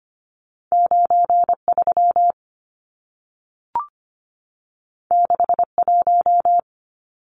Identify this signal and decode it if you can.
Morse code, heard as a keyed tone, about 25 wpm, 700 Hz. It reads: 93 61
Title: Numbers - 2-Digit 25wpm